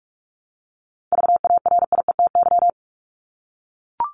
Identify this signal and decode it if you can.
Morse code, heard as a keyed tone, about 44 wpm, 700 Hz. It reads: VARIETY
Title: Top 501-750 Words 45wpm